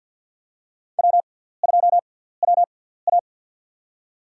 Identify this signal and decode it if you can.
Morse code, heard as a keyed tone, about 52 wpm, spaced as for 17 wpm, 690 Hz. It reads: W2WA
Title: USA Call Signs - 6x Character Spacing - 50wpm